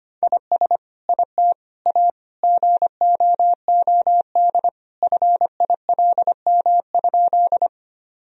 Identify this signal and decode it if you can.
Morse code, heard as a keyed tone, about 25 wpm, 700 Hz. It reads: IS IT A GOOD FILM?
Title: Sentences from Top 1000 Words - Review Part 3 - 25wpm